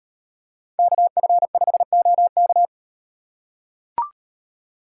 Morse code, heard as a keyed tone, about 33 wpm, 700 Hz. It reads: KF5OK